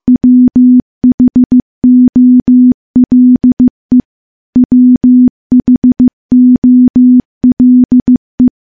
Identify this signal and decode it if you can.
Morse code, heard as a keyed tone, about 15 wpm, 250 Hz. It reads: WHOLE WHOLE
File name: morse / morse-training-15WPM